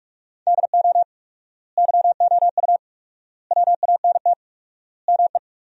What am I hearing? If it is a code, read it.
Morse code, heard as a keyed tone, about 42 wpm, 700 Hz. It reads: DO YOU WANT ME
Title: 4x Word-Spacing - Sentences from Top 200 Words - 45wpm